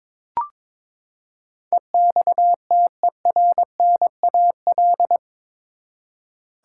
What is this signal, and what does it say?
Morse code, heard as a keyed tone, about 22 wpm, 700 Hz. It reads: EXTERNAL
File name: practice / 22wpm / Eight Letter Words - 22wpm